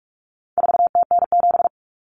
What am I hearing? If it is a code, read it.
Morse code, heard as a keyed tone, about 44 wpm, 700 Hz. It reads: 4TN7
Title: Common Cut Numbers - 4-Digit 45wpm